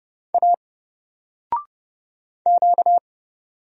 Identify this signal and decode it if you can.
Morse code, heard as a keyed tone, about 30 wpm, 700 Hz. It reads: A Q
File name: Single-Letter-Number-30wpm